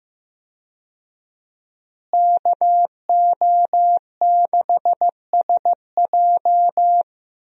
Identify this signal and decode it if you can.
Morse code, heard as a keyed tone, about 15 wpm, 700 Hz. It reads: KO6SJ
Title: USA Call Signs - 5 Characters - Repeated 2x - 15wpm